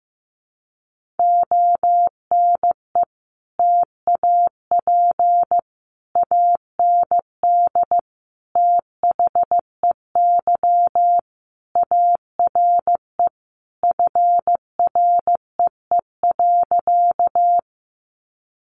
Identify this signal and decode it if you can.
Morse code, heard as a keyed tone, about 15 wpm, 700 Hz. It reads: ONE TAP AND THEY ARE FREE.
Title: Sentences from Top 1400 Words 15wpm